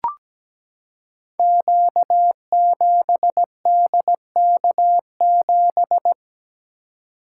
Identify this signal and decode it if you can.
Morse code, heard as a keyed tone, about 17 wpm, 700 Hz. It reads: Q7DK7
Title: Code Groups - 5 Letter:Numbers - 17wpm